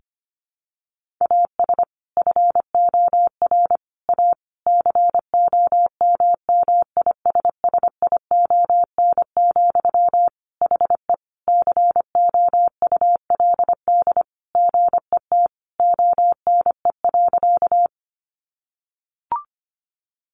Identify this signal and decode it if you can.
Morse code, heard as a keyed tone, about 25 wpm, 700 Hz. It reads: AS FOR A COMMISSION, HE COULD GET ONE.
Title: Sentences from Top 1400 Words 25wpm